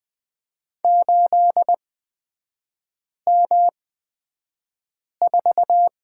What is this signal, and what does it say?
Morse code, heard as a keyed tone, about 20 wpm, spaced as for 5 wpm, 700 Hz. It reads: 8M4
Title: 243 Lesson 39e - Letter z - Three Characters 20wpm